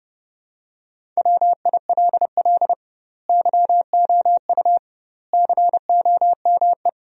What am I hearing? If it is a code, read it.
Morse code, heard as a keyed tone, about 30 wpm, 700 Hz. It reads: WILL YOU COME
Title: 2x Word-Spacing - Sentences from Top 200 Words - 30wpm